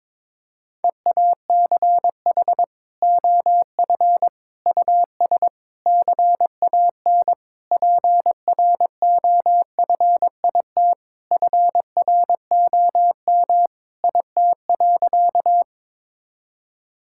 Morse code, heard as a keyed tone, about 22 wpm, 700 Hz. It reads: EACH OF US CAN PROFIT FROM IT.
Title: Sentences from Top 1000 Words - Review Part 5 - 22wpm